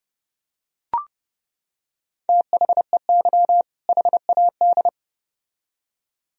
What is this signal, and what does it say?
Morse code, heard as a keyed tone, about 31 wpm, 700 Hz. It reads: THEY HAD